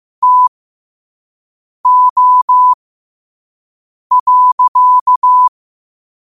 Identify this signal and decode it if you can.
Morse code, heard as a keyed tone, about 15 wpm, spaced as for 5 wpm, 1000 Hz. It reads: TO.
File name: qotdfiles / 5wpm